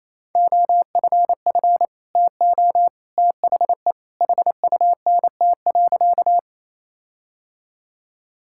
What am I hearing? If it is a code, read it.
Morse code, heard as a keyed tone, about 28 wpm, 700 Hz. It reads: OFF TO THE HUNT.
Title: Sentences from Top 1600 Words 28wpm